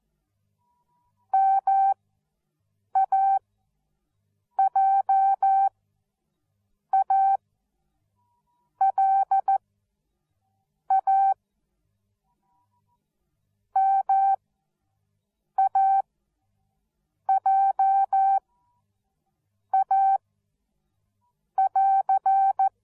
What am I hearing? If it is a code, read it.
Morse code, heard as a keyed tone, about 14 wpm, spaced as for 6 wpm, 780 Hz. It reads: MAJALA MAJA <AR>